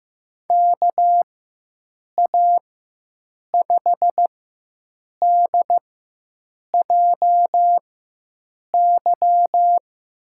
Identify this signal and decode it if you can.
Morse code, heard as a keyed tone, about 15 wpm, spaced as for 7 wpm, 700 Hz. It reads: KA5DJY
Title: USA Call Signs - 6 Characters - 4x Character Spacing - 15wpm